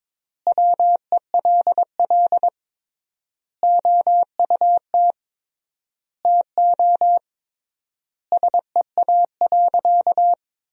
Morse code, heard as a keyed tone, about 22 wpm, 700 Hz. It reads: WELL OUT TO SEA.
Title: 3x Word-Spacing - Sentences from Top 400 Words - 22wpm